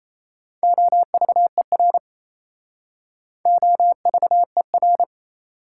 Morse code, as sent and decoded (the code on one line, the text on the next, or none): --- ...- . .-. / --- ...- . .-.
OVER OVER